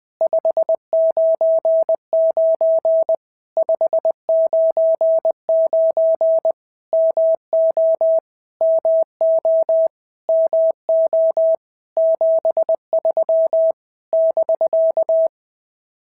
Morse code, heard as a keyed tone, about 20 wpm, 650 Hz. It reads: 599 599 MO MO MO 73 <BK>
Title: POTA QSO Practice - 20wpm